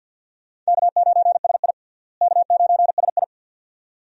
Morse code, heard as a keyed tone, about 50 wpm, 700 Hz. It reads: K9SI K9SI
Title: USA Call Signs - 4 Characters - Repeated 2x - 50wpm